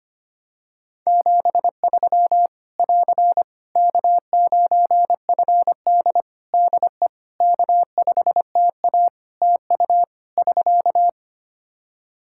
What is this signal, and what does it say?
Morse code, heard as a keyed tone, about 25 wpm, 700 Hz. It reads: 73 <AR> K9FD DE K5TA TU <SK>
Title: K1USN SST QSO Practice 25wpm